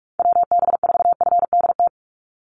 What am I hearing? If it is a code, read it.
Morse code, heard as a keyed tone, about 44 wpm, 700 Hz. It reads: WB4FDT
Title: Call Signs Repeated 2x - 45wpm